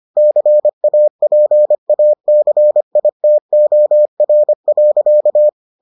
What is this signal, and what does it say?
Morse code, heard as a keyed tone, about 25 wpm, 600 Hz. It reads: CAPACITOR.